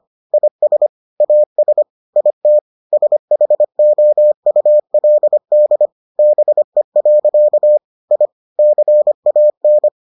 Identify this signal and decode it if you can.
Morse code, heard as a keyed tone, about 25 wpm, 600 Hz. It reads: IS AS IT SHOULD BE. I CAN